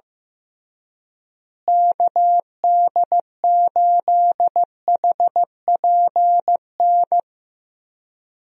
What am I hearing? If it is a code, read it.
Morse code, heard as a keyed tone, about 15 wpm, 700 Hz. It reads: KD8HPN
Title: USA Call Signs - 6 Characters - Repeated 3x - 15wpm